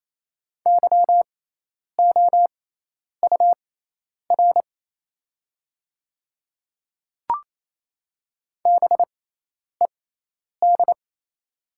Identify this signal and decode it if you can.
Morse code, heard as a keyed tone, about 28 wpm, spaced as for 10 wpm, 700 Hz. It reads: YOUR BED
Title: 6x Character Spacing - Top 500 Words - 28wpm